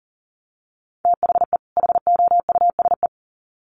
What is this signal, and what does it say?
Morse code, heard as a keyed tone, about 40 wpm, 700 Hz. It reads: THE HOUSE
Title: Sets of 2 Words 40wpm